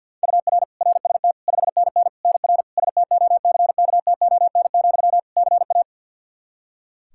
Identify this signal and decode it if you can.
Morse code, heard as a keyed tone, about 52 wpm, 700 Hz. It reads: UR RST 5NN NR STOCKTON, CA
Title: Practice QSO - Exchange 1 - Beginning - 50wpm